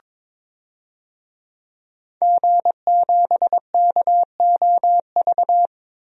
Morse code, heard as a keyed tone, about 22 wpm, 700 Hz. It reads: G7KOV
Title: DXCC-100 Call Signs - 5 Characters - 22wpm